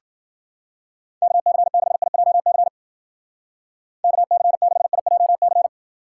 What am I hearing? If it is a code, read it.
Morse code, heard as a keyed tone, about 49 wpm, 700 Hz. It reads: KC6IJC KC6IJC
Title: Speed Racing - USA Call Signs 40wpm